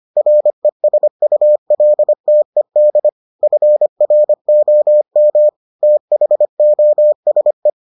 Morse code, heard as a keyed tone, about 25 wpm, 600 Hz. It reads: RESULTED FROM THOSE